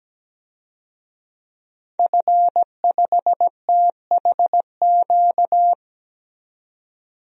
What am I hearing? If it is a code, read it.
Morse code, heard as a keyed tone, about 17 wpm, 700 Hz. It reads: F5THQ
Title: DXCC-100 Call Signs - 5 Characters - 17wpm